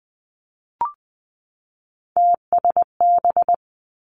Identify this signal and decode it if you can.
Morse code, heard as a keyed tone, about 19 wpm, 700 Hz. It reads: TSB
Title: Three Challenging Letter-Number Combinations 20wpm